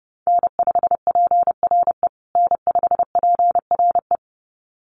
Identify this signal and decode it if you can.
Morse code, heard as a keyed tone, about 30 wpm, 700 Hz. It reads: N5PRE N5PRE
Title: USA Call Signs Repeated 2x 30wpm